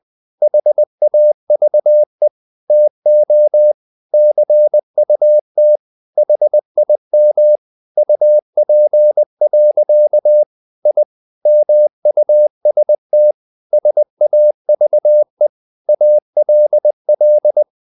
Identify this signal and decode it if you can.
Morse code, heard as a keyed tone, about 20 wpm, 600 Hz. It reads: HAVE TO CUT HIM UP. I MUST SAVE ALL
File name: the old man and the sea / the old man and the sea part 1 20wpm